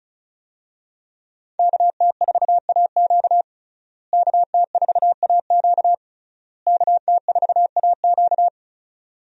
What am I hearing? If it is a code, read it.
Morse code, heard as a keyed tone, about 36 wpm, 700 Hz. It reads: KT4AQ KT4AQ KT4AQ
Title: USA Call Signs - 5 Characters - Repeated 3x - 35wpm